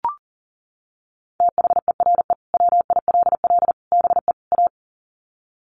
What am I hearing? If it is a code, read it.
Morse code, heard as a keyed tone, about 40 wpm, 700 Hz. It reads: THERE WILL BE A
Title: Sets of 4 Words 40wpm